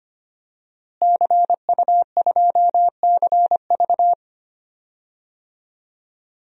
Morse code, heard as a keyed tone, about 25 wpm, 700 Hz. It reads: CU2CV
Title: DXCC-100 Call Signs - 5 Characters - Repeated 3x - 25wpm